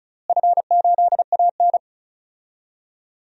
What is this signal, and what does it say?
Morse code, heard as a keyed tone, about 34 wpm, 700 Hz. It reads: F8AN
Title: DXCC-100 Call Signs - 4 Characters - Repeated 2x - 35wpm